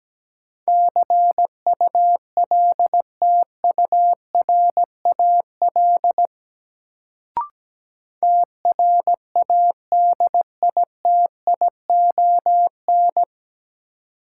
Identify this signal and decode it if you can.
Morse code, heard as a keyed tone, about 17 wpm, 700 Hz. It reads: CULTURAL TRADITION
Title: Sentences from Top 1500 Words - New Word Drill - 17wpm